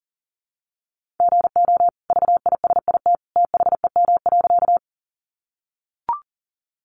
Morse code, heard as a keyed tone, about 40 wpm, 700 Hz. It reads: GO VISIT THEM.
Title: Sentences from Top 900 Words 40wpm